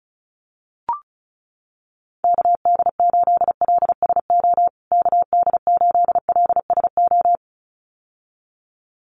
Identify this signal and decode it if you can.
Morse code, heard as a keyed tone, about 34 wpm, 700 Hz. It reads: KD8LSO KD8LSO
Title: USA Call Signs Repeated 2x 35wpm